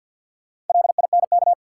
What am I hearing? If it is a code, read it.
Morse code, heard as a keyed tone, about 52 wpm, 700 Hz. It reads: LINK
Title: Sentences from Top 1500 Words - New Word Drill Alt - 50wpm